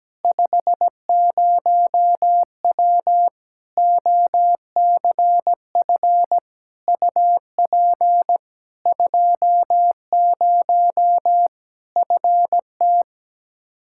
Morse code, heard as a keyed tone, about 17 wpm, 700 Hz. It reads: 50W OCF UP 20 FT
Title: Practice QSO - Exchange 2 - Info - 17wpm